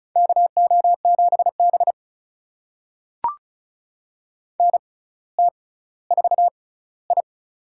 Morse code, heard as a keyed tone, about 36 wpm, spaced as for 18 wpm, 700 Hz. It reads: KO7B NT4I
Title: USA Call Signs - 4 Characters - 6x Character Spacing - 35wpm